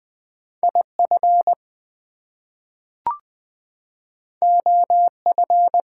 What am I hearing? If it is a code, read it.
Morse code, heard as a keyed tone, about 20 wpm, 700 Hz. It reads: IF OF
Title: Two Letter Words - 20wpm